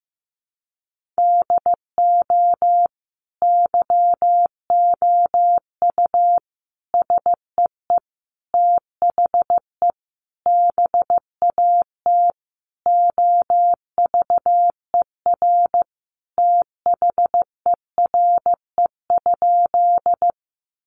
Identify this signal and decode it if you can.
Morse code, heard as a keyed tone, about 15 wpm, 700 Hz. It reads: DO YOU SEE THE BAT OVER THERE?
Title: Sentences from Top 1500 Words 15wpm